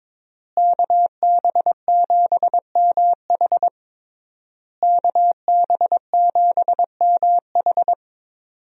Morse code, heard as a keyed tone, about 22 wpm, 700 Hz. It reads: KB7MH KB7MH
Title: USA Call Signs - 5 Characters - Repeated 3x - 22wpm